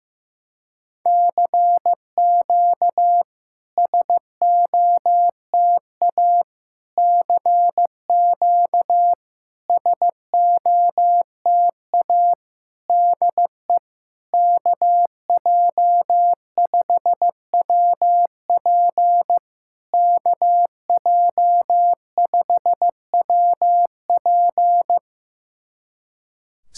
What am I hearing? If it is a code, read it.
Morse code, heard as a keyed tone, about 15 wpm, 700 Hz. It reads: CQ SOTA CQ SOTA DE KJ5WP KJ5WP
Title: SOTA QSO Practice 15wpm